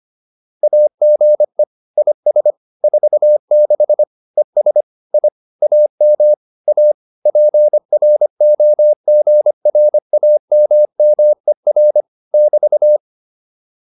Morse code, heard as a keyed tone, about 25 wpm, 600 Hz. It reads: AGE IS 46 ES I AM A PROGRAMMER <BT>